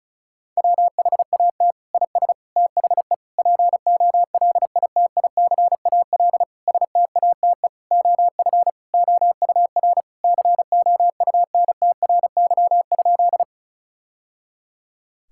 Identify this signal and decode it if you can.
Morse code, heard as a keyed tone, about 34 wpm, 700 Hz. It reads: WHAT IS THE POLITICAL STATE OF OUR COUNTRY?
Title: Sentences from Top 1000 Words - Review Part 5 - 35wpm